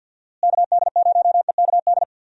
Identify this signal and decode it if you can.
Morse code, heard as a keyed tone, about 52 wpm, 700 Hz. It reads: KD0EKD
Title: USA Call Signs - 6 Characters - Repeated 3x - 50wpm